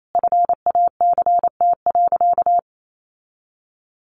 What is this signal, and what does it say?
Morse code, heard as a keyed tone, about 28 wpm, 700 Hz. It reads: FACT.
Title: Sentences from Top 600 Words 28wpm